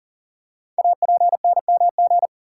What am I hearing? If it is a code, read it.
Morse code, heard as a keyed tone, about 38 wpm, 700 Hz. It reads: APNMG
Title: Code Groups - 5 Letters - 40wpm